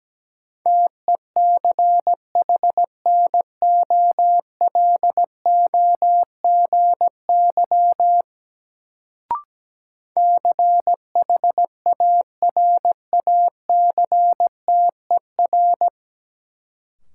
Morse code, heard as a keyed tone, about 17 wpm, 700 Hz. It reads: TECHNOLOGY CHARACTER
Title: Sentences from Top 1200 Words - New Word Drill Alt - 17wpm